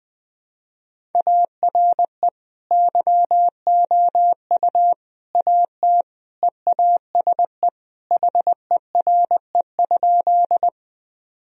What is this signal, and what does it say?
Morse code, heard as a keyed tone, about 20 wpm, 700 Hz. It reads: ARE YOU AT EASE HERE?
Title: Sentences from Top 1000 Words - Review Part 1 - 20wpm